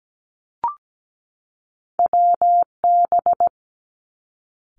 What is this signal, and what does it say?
Morse code, heard as a keyed tone, about 17 wpm, 700 Hz. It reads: WB
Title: Two Challenging Letter-Number Combinations 17wpm